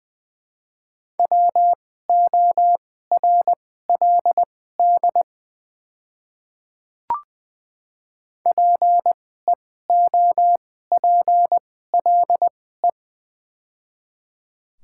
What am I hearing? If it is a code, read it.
Morse code, heard as a keyed tone, about 20 wpm, spaced as for 14 wpm, 690 Hz. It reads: WORLD PEOPLE
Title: 2x Character Spacing - Top 200 Words - 20wpm